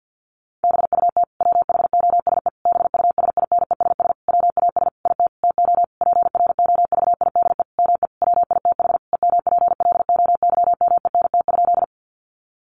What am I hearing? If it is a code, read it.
Morse code, heard as a keyed tone, about 50 wpm, 700 Hz. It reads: BUT WHOSE BUSINESS WAS IT TO PROVIDE ME WITH EMPLOYMENT?
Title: Sentences from Top 1500 Words 50wpm